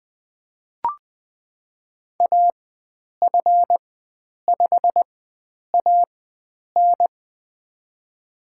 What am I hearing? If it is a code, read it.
Morse code, heard as a keyed tone, about 20 wpm, spaced as for 9 wpm, 700 Hz. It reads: AF5AN
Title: USA Call Signs - 4x Character Spacing - 20wpm